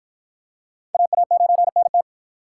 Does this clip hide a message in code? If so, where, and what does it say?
Yes: Morse code from 0.9 to 2.0 s, heard as a keyed tone, about 53 wpm, 700 Hz. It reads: AA9NT